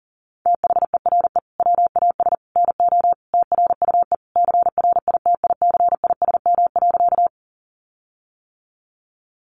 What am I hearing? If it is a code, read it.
Morse code, heard as a keyed tone, about 40 wpm, 700 Hz. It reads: THERE WAS NO TRUE CRITICISM.
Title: Sentences from Top 1300 Words 40wpm